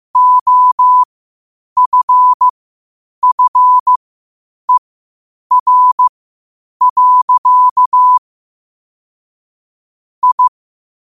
Morse code, heard as a keyed tone, about 15 wpm, spaced as for 8 wpm, 1000 Hz. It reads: OFFER. I
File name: qotdfiles / 8wpm